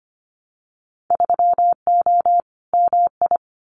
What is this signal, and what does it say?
Morse code, heard as a keyed tone, about 25 wpm, 700 Hz. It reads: 3O MI
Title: Winter Field Day Exchanges Encore 25wpm